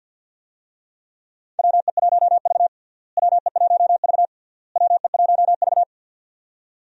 Morse code, heard as a keyed tone, about 51 wpm, 700 Hz. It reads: WE1V WE1V WE1V
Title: USA Call Signs - 4 Characters - Repeated 3x - 50wpm